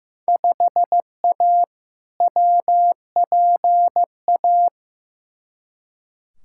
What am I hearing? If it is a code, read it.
Morse code, heard as a keyed tone, about 15 wpm, 700 Hz. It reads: 5A WPA
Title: Field Day Exchanges Encore - 15wpm